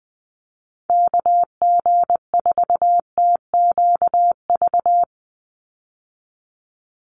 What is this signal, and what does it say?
Morse code, heard as a keyed tone, about 20 wpm, 700 Hz. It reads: KG4TQV
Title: USA Call Signs 20wpm